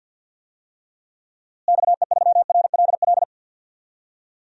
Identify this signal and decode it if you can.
Morse code, heard as a keyed tone, about 52 wpm, 700 Hz. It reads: XE3RLL
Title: DXCC-100 Call Signs - 6 Characters - Repeated 3x - 50wpm